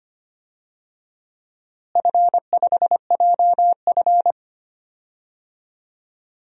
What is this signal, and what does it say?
Morse code, heard as a keyed tone, about 25 wpm, 700 Hz. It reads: F5JF
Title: DXCC-100 Call Signs - 4 Characters - Repeated 3x - 25wpm